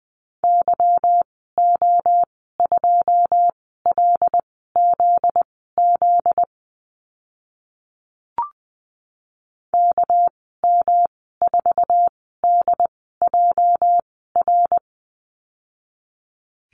Morse code, heard as a keyed tone, about 20 wpm, spaced as for 15 wpm, 700 Hz. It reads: YO2LZZ KM4DJR
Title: 245 Lesson 39g - Letter z - Call Signs 20wpm